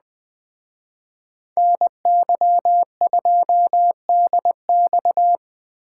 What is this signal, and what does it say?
Morse code, heard as a keyed tone, about 20 wpm, 700 Hz. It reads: NY2DX